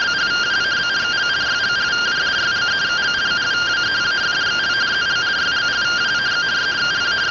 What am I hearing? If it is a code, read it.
Two-tone teleprinter signal, about 170 Hz shift.